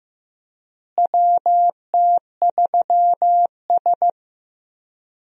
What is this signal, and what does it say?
Morse code, heard as a keyed tone, about 15 wpm, 700 Hz. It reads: WT3S
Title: USA Call Signs - 4 Characters - Repeated 3x - 15wpm